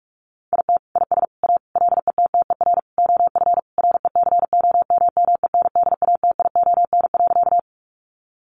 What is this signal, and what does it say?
Morse code, heard as a keyed tone, about 44 wpm, 700 Hz. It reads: IT IS A LETTER OF RECOMMENDATION.
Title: Sentences from Top 1400 Words 45wpm